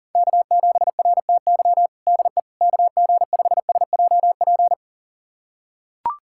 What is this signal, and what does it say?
Morse code, heard as a keyed tone, about 40 wpm, 700 Hz. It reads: K7RTY DE KG5SJP
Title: USA Callsign DE Callsign 40wpm